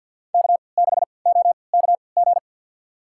Morse code, heard as a keyed tone, about 50 wpm, spaced as for 28 wpm, 700 Hz. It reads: K6OKG
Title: USA Call Signs - 5 Characters - 3x Character Spacing - 50wpm